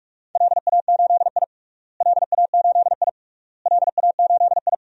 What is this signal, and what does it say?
Morse code, heard as a keyed tone, about 46 wpm, 700 Hz. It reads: LA8I LA8I LA8I
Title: DXCC-100 Call Signs - 4 Characters - Repeated 3x - 45wpm